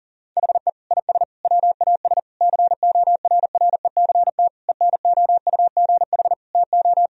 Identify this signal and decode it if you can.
Morse code, heard as a keyed tone, about 40 wpm, 700 Hz. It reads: HE IS WAS CORRECT ENOUGH TO
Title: Sentences from Top 1000 Words - Review Part 2 - 40wpm